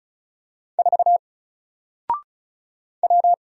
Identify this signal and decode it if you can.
Morse code, heard as a keyed tone, about 36 wpm, 700 Hz. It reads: 4 W
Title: Single Letter-Number 35wpm